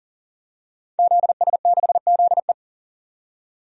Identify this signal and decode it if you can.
Morse code, heard as a keyed tone, about 40 wpm, 700 Hz. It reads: ZS6ZE
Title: DXCC-100 Call Signs - 5 Characters - 40wpm